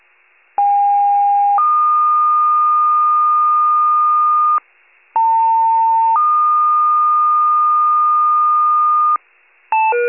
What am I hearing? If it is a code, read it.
Two-tone page, heard as a one-second tone, then a three-second tone, about 810 then 1220 Hz, about 900 then 1220 Hz. It is heard from 0.6 to 9.2 s.
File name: wcalert